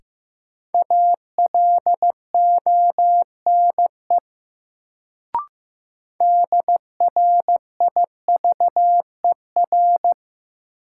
Morse code, heard as a keyed tone, about 15 wpm, 700 Hz. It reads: ALONE DRIVER
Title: Sentences from Top 1000 Words - New Word Drill - 15wpm